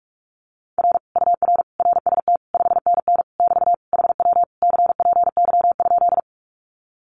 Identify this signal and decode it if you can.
Morse code, heard as a keyed tone, about 44 wpm, 700 Hz. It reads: R UR RST 5NN <BT> HW CPY?
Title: Common QSO Phrases Encore 45wpm